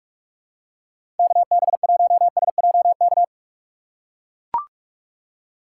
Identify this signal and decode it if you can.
Morse code, heard as a keyed tone, about 44 wpm, 700 Hz. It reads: KB1SJK
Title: USA Call Signs - 6 Characters - 6x Character Spacing - 45wpm